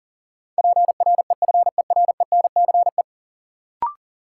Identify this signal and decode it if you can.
Morse code, heard as a keyed tone, about 40 wpm, 700 Hz. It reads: PREFERENCE
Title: Sentences from Top 1500 Words - New Word Drill Alt - 40wpm